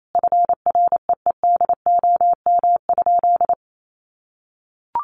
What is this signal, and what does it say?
Morse code, heard as a keyed tone, about 28 wpm, 700 Hz. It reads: FREEDOM?
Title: Sentences from Top 1500 Words 28wpm